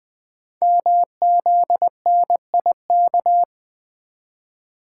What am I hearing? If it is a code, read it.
Morse code, heard as a keyed tone, about 20 wpm, 700 Hz. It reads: MZNIK
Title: Code Groups - 5 Letters - 20wpm